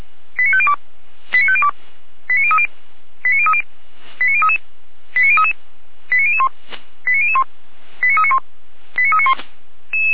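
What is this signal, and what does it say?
Five-tone selective call (ZVEI), heard as quick runs of five tones, tones of about 70 ms. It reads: 89502 89502 89030 89030 89033 89033 89001 89001 89391 89391
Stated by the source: ZVEI five-tone selective call 89140 received by SDR, from Feuerwehr Orlamünde (fire brigade).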